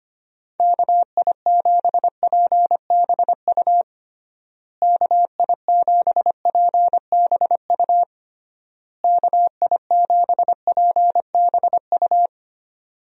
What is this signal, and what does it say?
Morse code, heard as a keyed tone, about 25 wpm, 700 Hz. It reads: KI7PBU KI7PBU KI7PBU
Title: USA Call Signs - 6 Characters - Repeated 3x - 25wpm